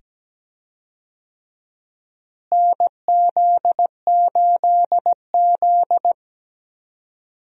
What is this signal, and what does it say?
Morse code, heard as a keyed tone, about 17 wpm, 700 Hz. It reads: NZ8Z
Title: USA Call Signs - 4 Characters - Repeated 2x - 17wpm